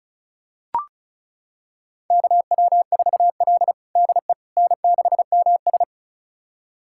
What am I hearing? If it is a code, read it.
Morse code, heard as a keyed tone, about 34 wpm, 700 Hz. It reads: KW4L DE N6MS
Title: USA Callsign DE Callsign - 4 Characters - 35wpm